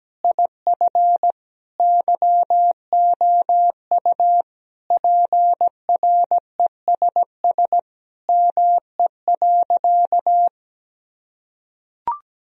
Morse code, heard as a keyed tone, about 17 wpm, 700 Hz. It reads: IF YOU PRESS ME.
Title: Sentences from Top 1000 Words - Review Part 1 - 17wpm